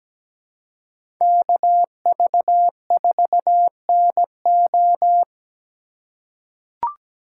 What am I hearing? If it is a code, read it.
Morse code, heard as a keyed tone, about 17 wpm, 700 Hz. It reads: KV4NO
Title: USA Call Signs - 5 Characters - 3x Character Spacing - 17wpm